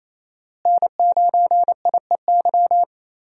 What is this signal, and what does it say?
Morse code, heard as a keyed tone, about 28 wpm, 700 Hz. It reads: N9IEY
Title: USA Call Signs - 5 Characters - Repeated 2x - 28wpm